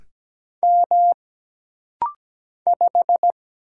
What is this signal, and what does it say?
Morse code, heard as a keyed tone, about 17 wpm, 700 Hz. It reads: M 5
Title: Single Letter-Number - Rapid-Fire 17wpm